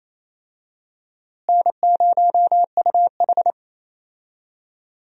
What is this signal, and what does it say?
Morse code, heard as a keyed tone, about 28 wpm, 700 Hz. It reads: N0UH
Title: USA Call Signs - 4 Characters - Repeated 2x - 28wpm